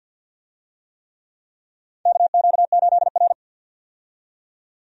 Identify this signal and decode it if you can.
Morse code, heard as a keyed tone, about 49 wpm, 700 Hz. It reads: KQ8R
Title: USA Call Signs - 4 Characters - 50wpm